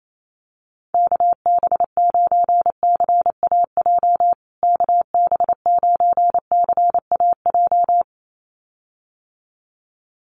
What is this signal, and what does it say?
Morse code, heard as a keyed tone, about 28 wpm, 700 Hz. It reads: KB9CAJ KB9CAJ
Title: USA Call Signs Repeated 2x 28wpm